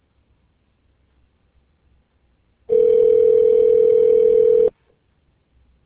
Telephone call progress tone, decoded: ringback tone

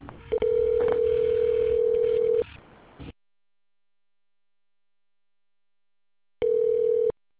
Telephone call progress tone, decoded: ringback tone

